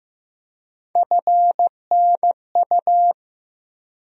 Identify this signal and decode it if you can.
Morse code, heard as a keyed tone, about 15 wpm, 700 Hz. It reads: FNU
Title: USA Call Sign Postfixes - 3 Letters 15wpm